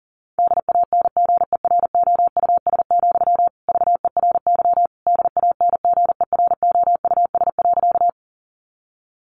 Morse code, heard as a keyed tone, about 40 wpm, 700 Hz. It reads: DANGEROUS, VERY DANGEROUS.
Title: Sentences from Top 1200 Words 40wpm